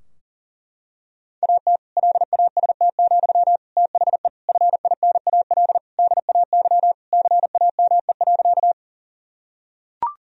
Morse code, heard as a keyed tone, about 40 wpm, 700 Hz. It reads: AT LAST, THE FINAL DAY CAME.
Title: Sentences from Top 1000 Words - Review Part 2 - 40wpm